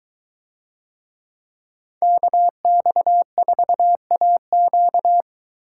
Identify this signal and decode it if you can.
Morse code, heard as a keyed tone, about 22 wpm, 700 Hz. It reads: KX4AQ